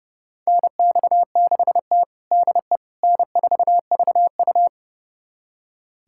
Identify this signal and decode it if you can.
Morse code, heard as a keyed tone, about 32 wpm, 700 Hz. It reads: NX6T DE N4VU
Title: USA Callsign DE Callsign - 4 Characters - 30wpm